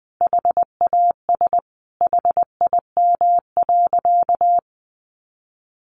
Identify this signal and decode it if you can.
Morse code, heard as a keyed tone, about 20 wpm, 700 Hz. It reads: HAS HIM.
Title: Sentences from Top 800 Words 20wpm